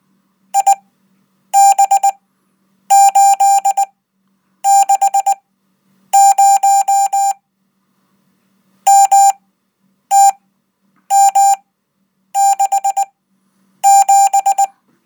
Morse code, heard as a keyed tone, about 19 wpm, spaced as for 8 wpm, 790 Hz. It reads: IB860 MTM67
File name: Code